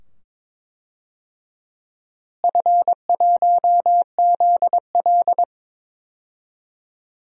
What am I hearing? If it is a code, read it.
Morse code, heard as a keyed tone, about 21 wpm, 700 Hz. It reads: F1ZL
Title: DXCC-100 Call Signs - 4 Characters - Repeated 2x - 22wpm